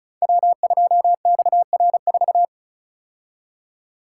Morse code, heard as a keyed tone, about 36 wpm, 700 Hz. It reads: W2XR4
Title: Code Groups - 5 Letter:Numbers - 35wpm